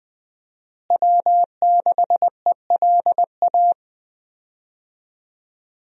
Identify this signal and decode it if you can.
Morse code, heard as a keyed tone, about 21 wpm, 700 Hz. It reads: W6ELA
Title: USA Call Signs - 5 Characters - Repeated 3x - 20wpm